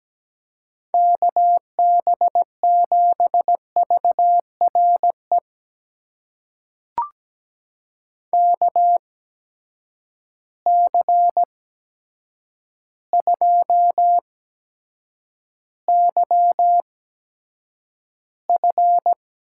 Morse code, heard as a keyed tone, about 17 wpm, spaced as for 8 wpm, 700 Hz. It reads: KB7VRE KC2YF
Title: USA Call Signs - 6 Characters - 8x Character Spacing - 17wpm